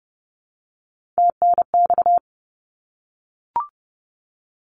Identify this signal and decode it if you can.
Morse code, heard as a keyed tone, about 30 wpm, 700 Hz. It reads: TNX
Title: Abbreviations Repeated 3x - 30wpm